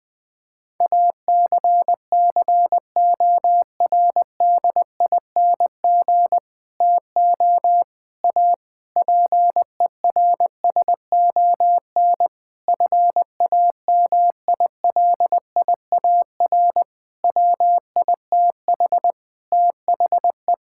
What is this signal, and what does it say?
Morse code, heard as a keyed tone, about 20 wpm, 700 Hz. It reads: ACCORDING TO A PERSON FAMILIAR WITH THE
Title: Sentences with 7-Word Phrases - 20wpm